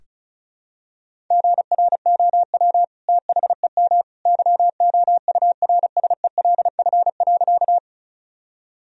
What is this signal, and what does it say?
Morse code, heard as a keyed tone, about 34 wpm, 700 Hz. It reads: GROW THEM YOURSELF.
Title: Sentences from Top 1000 Words - Review Part 1 - 35wpm